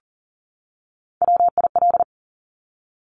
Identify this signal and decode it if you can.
Morse code, heard as a keyed tone, about 40 wpm, 690 Hz. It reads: WIL
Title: Top Three Letter Combinations in Words 40wpm